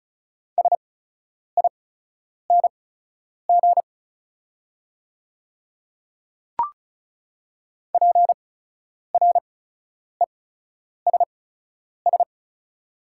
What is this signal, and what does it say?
Morse code, heard as a keyed tone, about 36 wpm, spaced as for 10 wpm, 700 Hz. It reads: SING PRESS
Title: 8x Character Spacing - Top 500 Words - 35wpm